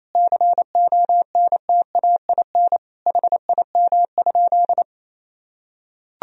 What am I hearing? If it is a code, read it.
Morse code, heard as a keyed tone, about 28 wpm, 700 Hz. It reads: CONTAIN HIM?
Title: Sentences from Top 1000 Words - Review Part 2 - 28wpm